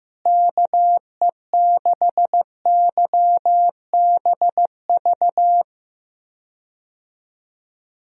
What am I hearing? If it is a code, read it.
Morse code, heard as a keyed tone, about 15 wpm, 700 Hz. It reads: KE6YBV
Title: USA Call Signs - 6 Characters - Repeated 3x - 15wpm